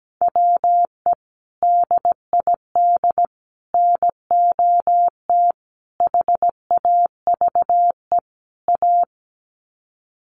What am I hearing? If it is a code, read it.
Morse code, heard as a keyed tone, about 17 wpm, 700 Hz. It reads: WE DID NOT HAVE A
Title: Sets of 5 Words Encore2 17wpm